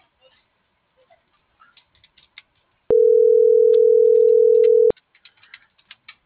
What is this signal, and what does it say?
Ringback tone (North America), 440 + 480 Hz.